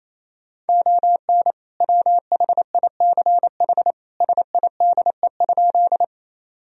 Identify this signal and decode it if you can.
Morse code, heard as a keyed tone, about 28 wpm, 700 Hz. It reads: ON WHICH SIDE?
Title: Sentences from Top 1000 Words - Review Part 1 - 28wpm